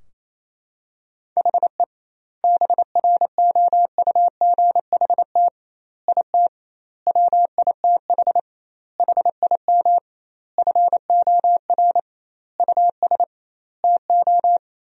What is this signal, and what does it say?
Morse code, heard as a keyed tone, about 28 wpm, 700 Hz. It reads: HE BROUGHT IT WITH HIM FOR US TO
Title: 2x Word-Spacing - Sentences from Top 500 Words - 28wpm